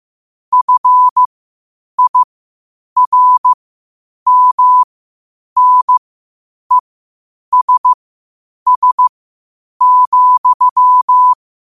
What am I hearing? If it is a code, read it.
Morse code, heard as a keyed tone, about 15 wpm, spaced as for 8 wpm, 1000 Hz. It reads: FIRMNESS,
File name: qotdfiles / 8wpm